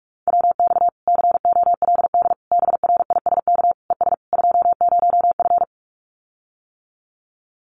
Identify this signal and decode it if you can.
Morse code, heard as a keyed tone, about 44 wpm, 700 Hz. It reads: WX COLD BRISK ES 20F
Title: Common QSO Phrases Encore 45wpm